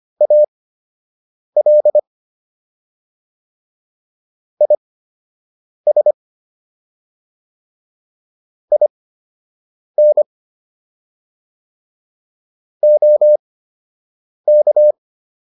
Morse code, heard as a keyed tone, about 25 wpm, spaced as for 7 wpm, 600 Hz. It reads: AL IS IN OK